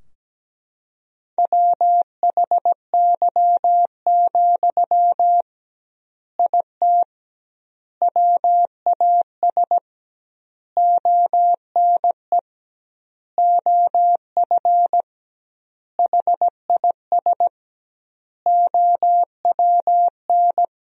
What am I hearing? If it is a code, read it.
Morse code, heard as a keyed tone, about 17 wpm, 700 Hz. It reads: WHY, IT WAS ONE OF HIS OWN
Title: 2x Word-Spacing - Sentences from Top 300 Words - 17wpm